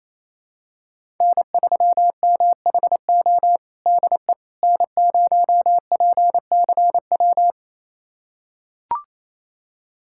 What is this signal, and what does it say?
Morse code, heard as a keyed tone, about 28 wpm, 700 Hz. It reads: N3MHO DE N0PCW
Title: USA Callsign DE Callsign - 5 Characters - 28wpm